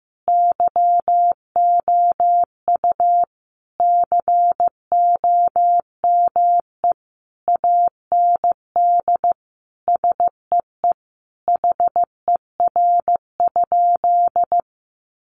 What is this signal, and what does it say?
Morse code, heard as a keyed tone, about 15 wpm, 700 Hz. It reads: YOU COME AND SEE HER?
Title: Sentences from Top 100 Words 15wpm